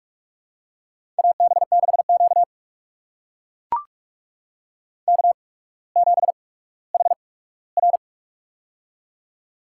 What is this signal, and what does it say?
Morse code, heard as a keyed tone, about 44 wpm, spaced as for 19 wpm, 700 Hz. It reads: AB6Q K7HR
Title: USA Call Signs - 8x Character Spacing - 45wpm